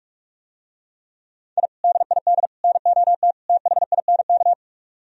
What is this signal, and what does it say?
Morse code, heard as a keyed tone, about 46 wpm, 700 Hz. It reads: I DID NOT THINK